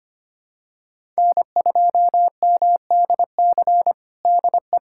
Morse code, heard as a keyed tone, about 25 wpm, 700 Hz. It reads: N2MDC DE
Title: USA Callsign DE Callsign - 5 Characters - 25wpm